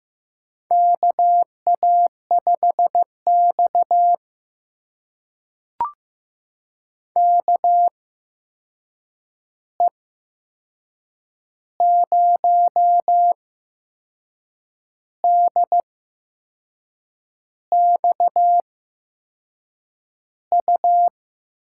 Morse code, heard as a keyed tone, about 15 wpm, spaced as for 5 wpm, 700 Hz. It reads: KA5XKE 0DXU
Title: USA Call Signs - 8x Character Spacing - 15wpm